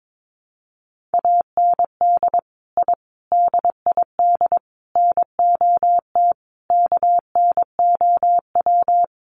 Morse code, heard as a keyed tone, about 22 wpm, 700 Hz. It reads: AND I DID NOT KNOW
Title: Sets of 5 Words Encore 22wpm